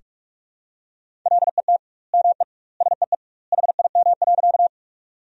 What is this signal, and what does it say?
Morse code, heard as a keyed tone, about 46 wpm, 700 Hz. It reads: LET ME SEE HIM.
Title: 2x Word-Spacing - Sentences from Top 300 Words - 45wpm